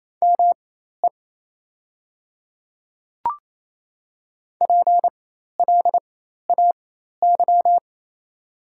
Morse code, heard as a keyed tone, about 28 wpm, spaced as for 13 wpm, 690 Hz. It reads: ME PLAY
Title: 4x Character Spacing - Top 200 Words - 28wpm